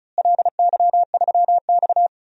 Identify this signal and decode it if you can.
Morse code, heard as a keyed tone, about 36 wpm, 700 Hz. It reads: LY3X
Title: DXCC-100 Call Signs - 4 Characters - Repeated 3x - 35wpm